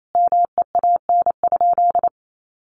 Morse code, heard as a keyed tone, about 28 wpm, 700 Hz. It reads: MEAN?
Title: Sentences from Top 1300 Words 28wpm